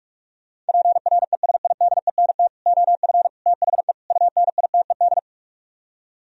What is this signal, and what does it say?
Morse code, heard as a keyed tone, about 58 wpm, 700 Hz. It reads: PRESIDENT OF THE UNITED